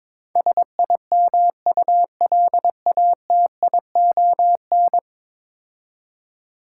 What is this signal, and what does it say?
Morse code, heard as a keyed tone, about 21 wpm, 700 Hz. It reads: SIMULATION
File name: practice / 22wpm / 10 Letter Words - First 3 Letters Given - 22wpm